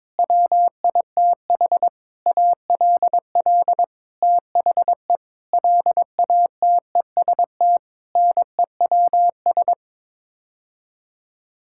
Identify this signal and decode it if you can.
Morse code, heard as a keyed tone, about 22 wpm, 700 Hz. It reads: WITH ALL THE LATEST NEWS
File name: Sets of 9 Words - 22wpm